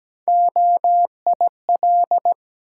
Morse code, heard as a keyed tone, about 17 wpm, 700 Hz. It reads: OIL
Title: Sentences from Top 1000 Words - Review Part 4 - New Word Drill - 17wpm